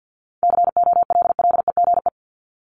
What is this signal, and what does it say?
Morse code, heard as a keyed tone, about 50 wpm, 700 Hz. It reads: COLLEGE
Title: Top 1001-1250 Words 50wpm